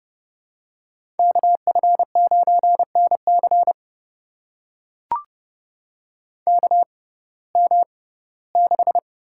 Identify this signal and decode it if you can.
Morse code, heard as a keyed tone, about 31 wpm, spaced as for 19 wpm, 700 Hz. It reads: KF9NC KM6
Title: USA Call Signs - 5 Characters - 6x Character Spacing - 30wpm